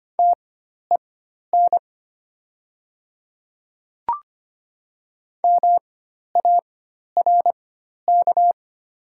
Morse code, heard as a keyed tone, about 25 wpm, spaced as for 12 wpm, 700 Hz. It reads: TEN MARK
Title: 4x Character Spacing - Top 500 Words - 25wpm